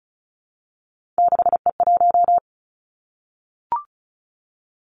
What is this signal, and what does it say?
Morse code, heard as a keyed tone, about 36 wpm, 700 Hz. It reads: 6E1
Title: Three Challenging Letter-Number Combinations 35wpm